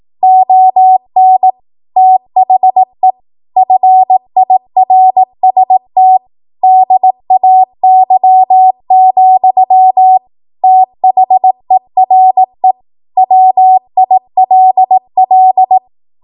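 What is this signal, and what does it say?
Morse code, heard as a keyed tone, about 18 wpm, 750 Hz. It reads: ON THE FIRST DAY, THERE WILL
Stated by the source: 18 WPM Morse code practice from a national amateur radio association's code archive.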